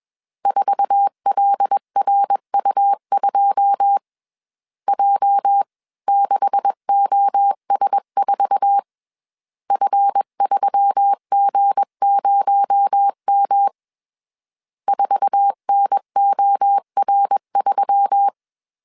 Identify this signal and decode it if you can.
Morse code, heard as a keyed tone, about 21 wpm, 800 Hz. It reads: 4LRU2 J6OS4 F3G0M 4NOR3